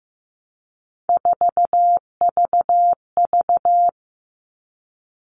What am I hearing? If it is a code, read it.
Morse code, heard as a keyed tone, about 15 wpm, 700 Hz. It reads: 4VV
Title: All Cut Numbers - 3-Digit 15wpm